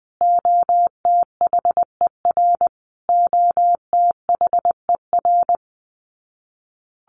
Morse code, heard as a keyed tone, about 20 wpm, 700 Hz. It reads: OTHER OTHER
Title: Top 200 Words Repeated 20wpm